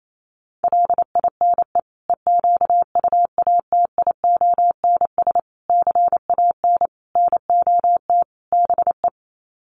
Morse code, heard as a keyed tone, about 28 wpm, 700 Hz. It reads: LINE EQUATIONS CAN NOT BE
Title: Sets of 5 Words 28wpm